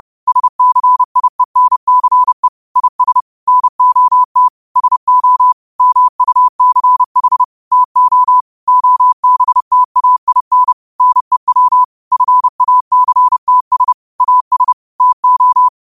Morse code, heard as a keyed tone, about 30 wpm, 1000 Hz. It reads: SCIENCE IS NOT SO MUCH TO OBTAIN NEW FACTS AS TO
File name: qotdfiles / 30wpm